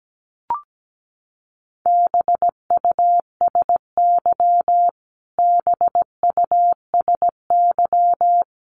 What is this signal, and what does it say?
Morse code, heard as a keyed tone, about 17 wpm, 700 Hz. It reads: BUSY BUSY
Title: Top 500 Words Repeated 17wpm